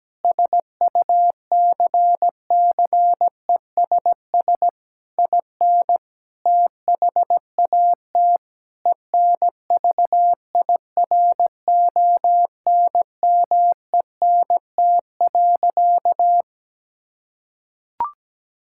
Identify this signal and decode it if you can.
Morse code, heard as a keyed tone, about 17 wpm, 700 Hz. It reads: SUCCESS IN THAT ENVIRONMENT.
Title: Sentences from Top 1000 Words - Review Part 5 - 17wpm